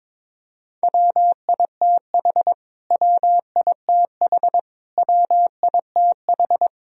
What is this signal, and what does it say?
Morse code, heard as a keyed tone, about 22 wpm, 700 Hz. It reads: WITH WITH WITH
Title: Top 100 Words Repeated 22wpm